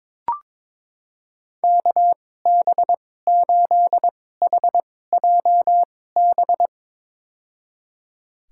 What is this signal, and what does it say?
Morse code, heard as a keyed tone, about 22 wpm, spaced as for 16 wpm, 700 Hz. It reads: KB8HJB
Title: USA Call Signs - 6 Characters - 2x Character Spacing - 22wpm